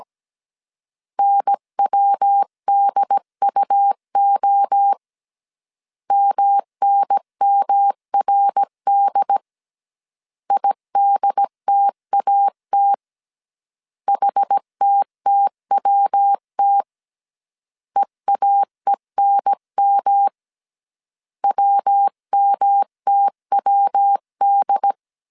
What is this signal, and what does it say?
Morse code, heard as a keyed tone, about 17 wpm, 800 Hz. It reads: NWDUO MNMRD IDTAT HTTWT EAENM WMTWD
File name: CW / DET / 1 3 1500C S A 16wpm 2141